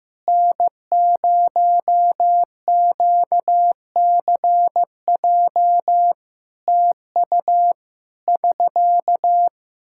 Morse code, heard as a keyed tone, about 16 wpm, 700 Hz. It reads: N0QCJ TU <SK>
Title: Practice QSO - Exchange 4 - Ending - 15wpm